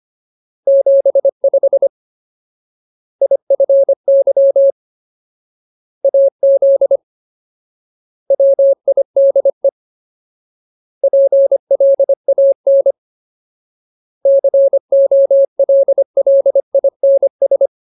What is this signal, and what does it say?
Morse code, heard as a keyed tone, about 25 wpm, 550 Hz. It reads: 75 IFY AZ WIDE PLAN COLLINS
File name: Ultimate Word List / 25WPM Ultimate